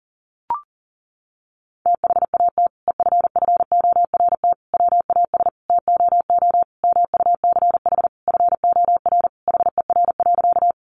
Morse code, heard as a keyed tone, about 40 wpm, 700 Hz. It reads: THAT EFFORT WAS TOO MUCH FOR HER.
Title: Sentences from Top 1200 Words 40wpm